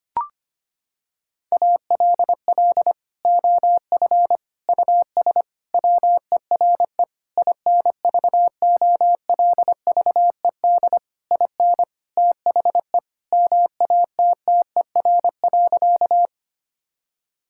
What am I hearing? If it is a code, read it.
Morse code, heard as a keyed tone, about 25 wpm, 700 Hz. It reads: ALL OF US WERE INVOLVED IN THE MATTER.
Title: Sentences from Top 1000 Words - Review Part 5 - 25wpm